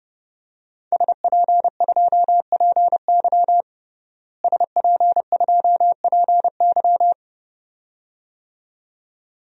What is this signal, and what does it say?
Morse code, heard as a keyed tone, about 30 wpm, 700 Hz. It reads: SP2PY SP2PY
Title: DXCC-100 Call Signs - 5 Characters - Repeated 2x - 30wpm